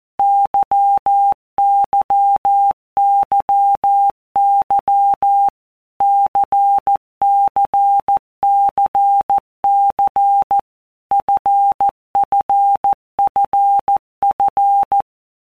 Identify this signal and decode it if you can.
Morse code, heard as a keyed tone, about 14 wpm, 800 Hz. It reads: YYYY CCCC FFFF